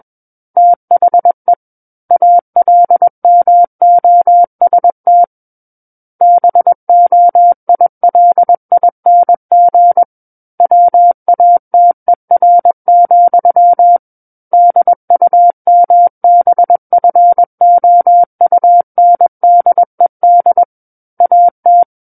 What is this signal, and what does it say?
Morse code, heard as a keyed tone, about 21 wpm, 700 Hz. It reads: THE ALMOST BOILING WATER, DUMBFOUNDED AT
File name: Book One / 12-Chapter 12 - 21 WPM